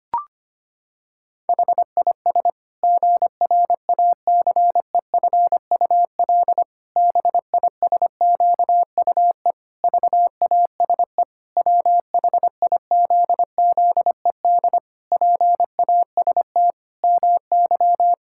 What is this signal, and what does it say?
Morse code, heard as a keyed tone, about 25 wpm, 700 Hz. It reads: HIS GRACEFUL BISQUE VASE WHIZZED PAST MY
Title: Pangram Challenge v2 - 25wpm